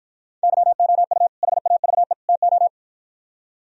Morse code, heard as a keyed tone, about 52 wpm, 700 Hz. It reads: YOU HAVE TO